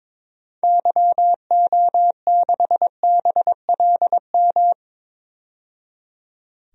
Morse code, heard as a keyed tone, about 22 wpm, 700 Hz. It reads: YO6BLM
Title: DXCC-100 Call Signs - 6 Characters - 22wpm